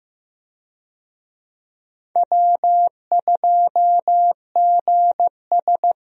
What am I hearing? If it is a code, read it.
Morse code, heard as a keyed tone, about 15 wpm, 700 Hz. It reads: W2GS
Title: USA Call Signs - 4 Characters - Repeated 2x - 15wpm